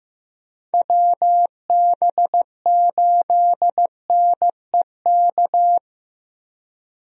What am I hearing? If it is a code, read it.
Morse code, heard as a keyed tone, about 16 wpm, 700 Hz. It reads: WB8NEK
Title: USA Call Signs - 6 Characters - Repeated 2x - 15wpm